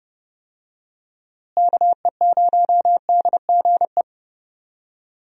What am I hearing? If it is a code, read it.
Morse code, heard as a keyed tone, about 30 wpm, 700 Hz. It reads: KE0DGE
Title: USA Call Signs - 6 Characters - Repeated 2x - 30wpm